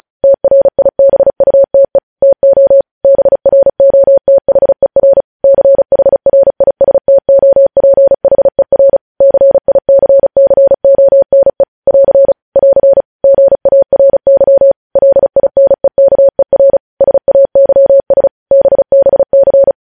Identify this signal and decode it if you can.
Morse code, heard as a keyed tone, about 36 wpm, 550 Hz. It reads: TRIBUTE TO BROTHER CHRISTOPHER CICCONE <AR> <AR> GARY LINEKER SAYS BBC